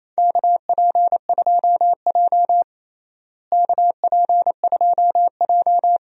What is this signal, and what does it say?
Morse code, heard as a keyed tone, about 29 wpm, 700 Hz. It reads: KP2J KP2J
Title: DXCC-100 Call Signs - 4 Characters - Repeated 2x - 28wpm